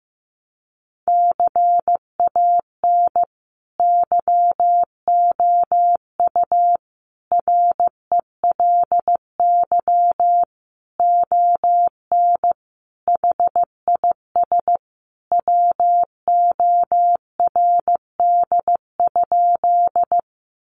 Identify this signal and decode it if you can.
Morse code, heard as a keyed tone, about 15 wpm, 700 Hz. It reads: CAN YOU RELY ON HIS WORD?
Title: Sentences from Top 1200 Words 15wpm